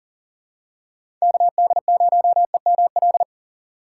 Morse code, heard as a keyed tone, about 40 wpm, 700 Hz. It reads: KD0EML
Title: USA Call Signs - 3x Character Spacing - 40wpm